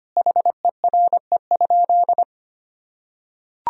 Morse code, heard as a keyed tone, about 25 wpm, 700 Hz. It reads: HERE?
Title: Sentences from Top 1000 Words - Review Part 1 - 25wpm